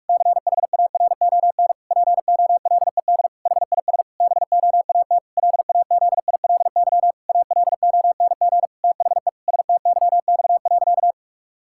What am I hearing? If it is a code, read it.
Morse code, heard as a keyed tone, about 46 wpm, 700 Hz. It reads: CHARON POLED HIS BOAT LAZILY ALONG THE STYX.
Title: Pangram Challenge v2 - 45wpm